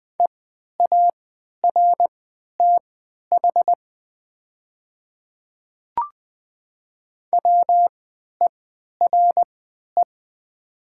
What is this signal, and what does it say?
Morse code, heard as a keyed tone, about 20 wpm, spaced as for 11 wpm, 690 Hz. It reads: EARTH WERE
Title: 3x Character Spacing - Top 200 Words - 20wpm